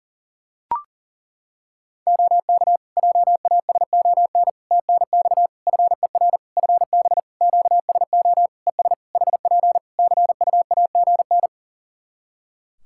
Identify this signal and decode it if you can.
Morse code, heard as a keyed tone, about 37 wpm, 700 Hz. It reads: OK JASON TNX FER FB QSO ES HP CUAGN
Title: Practice QSO - Exchange 4 - Ending - 40wpm